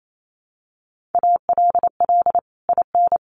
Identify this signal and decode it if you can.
Morse code, heard as a keyed tone, about 28 wpm, 700 Hz. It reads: ALL IN
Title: Sentences from Top 1500 Words 28wpm